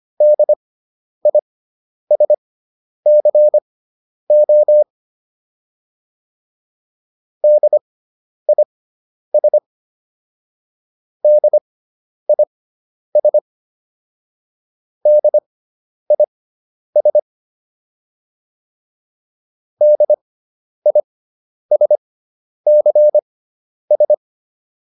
Morse code, heard as a keyed tone, about 25 wpm, spaced as for 10 wpm, 600 Hz. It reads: DISCO DIS DIS DIS DISCS